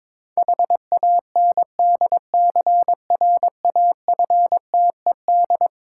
Morse code, heard as a keyed tone, about 22 wpm, 690 Hz. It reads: HANDCRAFTED
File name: Difficult Words with Hints - 22wpm